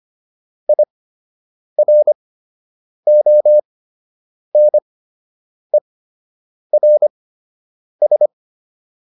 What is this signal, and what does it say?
Morse code, heard as a keyed tone, about 25 wpm, spaced as for 8 wpm, 600 Hz. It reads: IRONERS